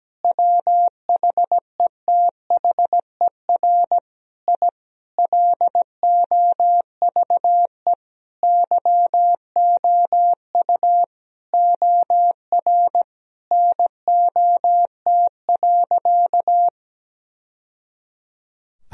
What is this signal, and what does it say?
Morse code, heard as a keyed tone, about 17 wpm, 700 Hz. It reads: WHETHER I LOVE YOU OR NOT.
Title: Sentences from Top 1000 Words - Review Part 3 - 17wpm